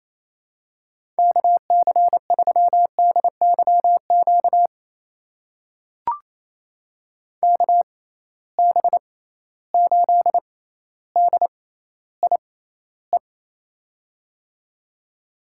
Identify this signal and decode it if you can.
Morse code, heard as a keyed tone, about 28 wpm, spaced as for 14 wpm, 700 Hz. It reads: KC3DYQ KB8DIE
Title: USA Call Signs - 6x Character Spacing - 28wpm